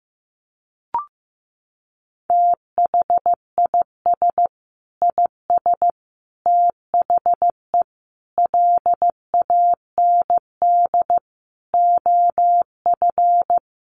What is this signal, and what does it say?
Morse code, heard as a keyed tone, about 15 wpm, 700 Hz. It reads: THIS IS THE LAND OF
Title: Sentences from Top 1300 Words 15wpm